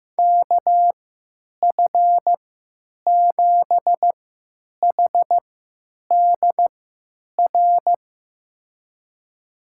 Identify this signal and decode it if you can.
Morse code, heard as a keyed tone, about 15 wpm, spaced as for 9 wpm, 700 Hz. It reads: KF7HDR
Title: USA Call Signs - 6 Characters - 3x Character Spacing - 15wpm